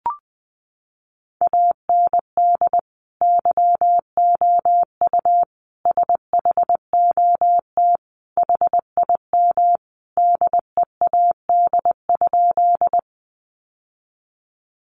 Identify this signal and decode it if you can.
Morse code, heard as a keyed tone, about 20 wpm, 700 Hz. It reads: AND YOU SHOT HIM DEAD?
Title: Sentences from Top 1500 Words 20wpm